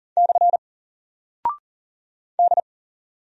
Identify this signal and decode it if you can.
Morse code, heard as a keyed tone, about 40 wpm, 700 Hz. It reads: / D
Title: Single Character (Letter:Number:Punctuation:Prosign) 40wpm